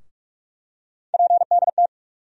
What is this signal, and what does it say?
Morse code, heard as a keyed tone, about 46 wpm, 700 Hz. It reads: PDT
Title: USA Call Sign Postfixes - 3 Letters 45wpm